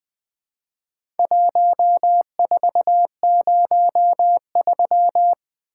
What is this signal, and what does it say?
Morse code, heard as a keyed tone, about 20 wpm, 700 Hz. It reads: 1403
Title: Four Letter-Number Combinations - 20wpm